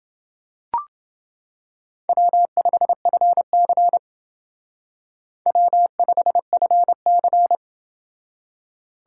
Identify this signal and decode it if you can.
Morse code, heard as a keyed tone, about 30 wpm, 700 Hz. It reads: W5FC W5FC